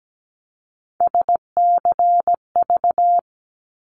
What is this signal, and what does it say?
Morse code, heard as a keyed tone, about 17 wpm, 700 Hz. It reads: SCV
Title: ARRL Sections 17wpm